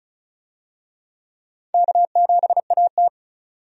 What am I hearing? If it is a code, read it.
Morse code, heard as a keyed tone, about 34 wpm, 700 Hz. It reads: K7AT
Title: USA Call Signs - 4 Characters - 35wpm